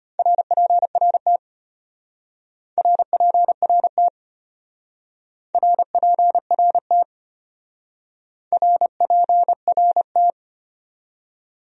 Morse code, heard as a keyed tone, about 30 wpm, 700 Hz. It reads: RPRT RPRT RPRT RPRT
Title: Speed-Racing Abbreviations - 25wpm